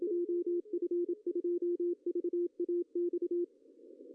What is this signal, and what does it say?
Morse code, heard as a keyed tone, about 27 wpm, 350 Hz, with radio noise. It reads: JF2VAX
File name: CW / cqww-cw-k5tr